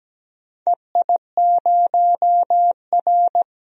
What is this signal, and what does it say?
Morse code, heard as a keyed tone, about 17 wpm, 700 Hz. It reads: EI0R
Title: DXCC-100 Call Signs - 4 Characters - 17wpm